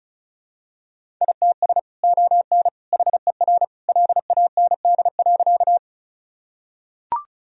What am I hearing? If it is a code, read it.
Morse code, heard as a keyed tone, about 36 wpm, 700 Hz. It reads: ITS ON HER LAND.
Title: Sentences from Top 1000 Words - Review Part 1 - 35wpm